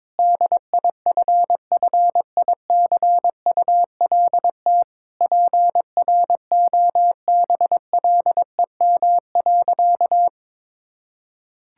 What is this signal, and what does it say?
Morse code, heard as a keyed tone, about 22 wpm, 700 Hz. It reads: DIFFICULT PROBLEM.
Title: Sentences from Top 1000 Words - Review Part 6 - 22wpm